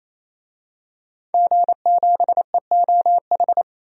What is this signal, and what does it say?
Morse code, heard as a keyed tone, about 28 wpm, 700 Hz. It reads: G7EOH
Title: DXCC-100 Call Signs - 5 Characters - 28wpm